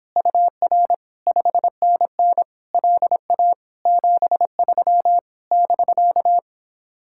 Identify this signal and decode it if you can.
Morse code, heard as a keyed tone, about 29 wpm, 700 Hz. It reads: UR 5NN LA 73 <BK>